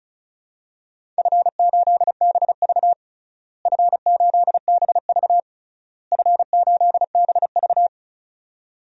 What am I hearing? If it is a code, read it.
Morse code, heard as a keyed tone, about 36 wpm, 700 Hz. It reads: F8BV F8BV F8BV
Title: DXCC-100 Call Signs - 4 Characters - Repeated 3x - 35wpm